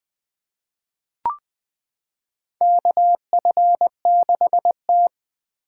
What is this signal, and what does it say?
Morse code, heard as a keyed tone, about 20 wpm, 700 Hz. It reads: KF6T
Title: K1USN SST QSO Practice - 20wpm